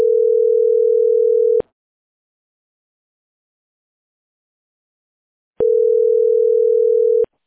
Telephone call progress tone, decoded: ringback tone